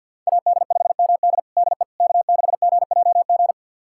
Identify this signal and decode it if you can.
Morse code, heard as a keyed tone, about 47 wpm, 700 Hz. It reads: AD5MD DE K6GJG
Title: USA Callsign DE Callsign - 5 Characters - 50wpm